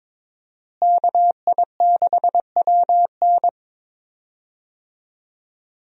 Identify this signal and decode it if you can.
Morse code, heard as a keyed tone, about 22 wpm, 700 Hz. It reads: KI6WN
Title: USA Call Signs - 5 Characters - Repeated 2x - 22wpm